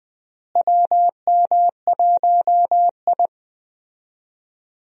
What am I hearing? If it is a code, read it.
Morse code, heard as a keyed tone, about 20 wpm, 700 Hz. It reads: WM1I
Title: USA Call Signs - 4 Characters - Repeated 3x - 20wpm